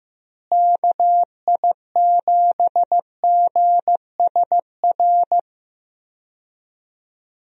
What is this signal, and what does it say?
Morse code, heard as a keyed tone, about 15 wpm, 700 Hz. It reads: KI7GSR
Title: USA Call Signs - 6 Characters - Repeated 3x - 15wpm